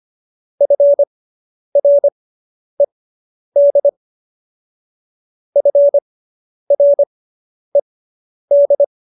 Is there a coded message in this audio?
Yes: Morse code from 0.6 to 8.8 s, heard as a keyed tone, about 25 wpm, spaced as for 10 wpm, 570 Hz. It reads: FRED FRED